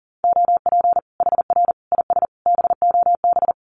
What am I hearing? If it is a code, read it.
Morse code, heard as a keyed tone, about 40 wpm, 700 Hz. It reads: OP HR IS BOB
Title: Common QSO Phrases 40wpm